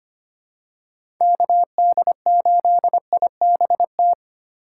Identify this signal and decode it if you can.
Morse code, heard as a keyed tone, about 25 wpm, 700 Hz. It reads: KD8IBT
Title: USA Call Signs - 6 Characters - 2x Character Spacing - 25wpm